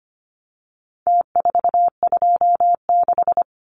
Morse code, heard as a keyed tone, about 25 wpm, 700 Hz. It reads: T426
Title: Numbers - Contest Style - 4-Digit 25wpm